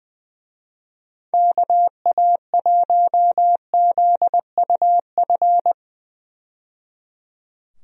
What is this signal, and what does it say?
Morse code, heard as a keyed tone, about 20 wpm, 700 Hz. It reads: KA1ZUF
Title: USA Call Signs - 6 Characters - 20wpm